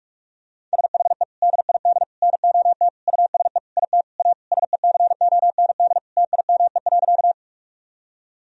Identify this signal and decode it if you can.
Morse code, heard as a keyed tone, about 43 wpm, 700 Hz. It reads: SHE DID NOT USE IT A SECOND TIME.
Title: Sentences from Top 1000 Words - Review Part 1 - 45wpm